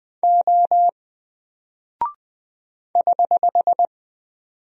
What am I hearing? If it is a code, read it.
Morse code, heard as a keyed tone, about 20 wpm, 700 Hz. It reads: O <HH>
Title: Single Character (Letter:Number:Punctuation:Prosign) Alt 20wpm